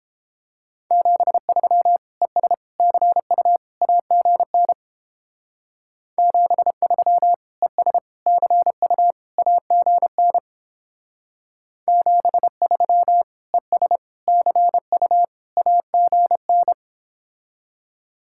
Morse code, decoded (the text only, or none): 73 ES CU AGN 73 ES CU AGN 73 ES CU AGN